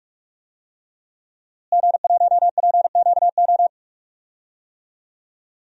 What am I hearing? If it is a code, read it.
Morse code, heard as a keyed tone, about 45 wpm, 700 Hz. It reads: G1PQO
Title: DXCC-100 Call Signs - 5 Characters - Repeated 3x - 45wpm